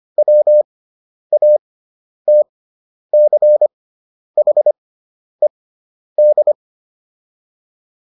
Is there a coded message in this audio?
Yes: Morse code from 0.2 to 6.5 s, heard as a keyed tone, about 25 wpm, spaced as for 10 wpm, 600 Hz. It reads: WATCHED